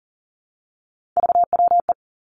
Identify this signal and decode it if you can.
Morse code, heard as a keyed tone, about 42 wpm, 700 Hz. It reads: VWE
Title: Three Challenging Letter-Number Combinations 40wpm